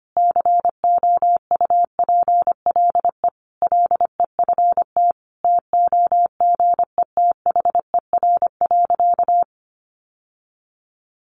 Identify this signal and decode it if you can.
Morse code, heard as a keyed tone, about 25 wpm, 700 Hz. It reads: COUPLE LEFT TOGETHER.
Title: Sentences from Top 1100 Words 25wpm